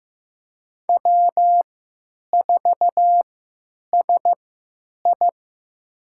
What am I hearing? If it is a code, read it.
Morse code, heard as a keyed tone, about 15 wpm, spaced as for 9 wpm, 700 Hz. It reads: W4SI
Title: USA Call Signs - 4 Characters - 3x Character Spacing - 15wpm